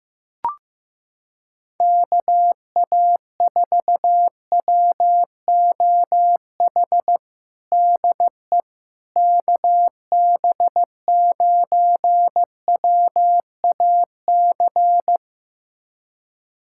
Morse code, heard as a keyed tone, about 15 wpm, 700 Hz. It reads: KA4WOH DE KB9WAC
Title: USA Callsign DE Callsign 15wpm